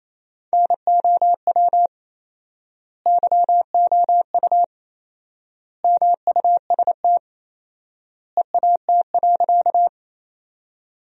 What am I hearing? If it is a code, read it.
Morse code, heard as a keyed tone, about 28 wpm, 700 Hz. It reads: NOW YOU MUST EAT.
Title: 4x Word-Spacing - Sentences from Top 300 Words - 28wpm